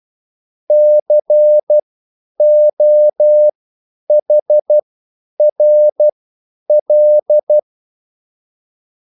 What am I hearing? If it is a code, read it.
Morse code, heard as a keyed tone, about 12 wpm, spaced as for 9 wpm, 600 Hz. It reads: COHRL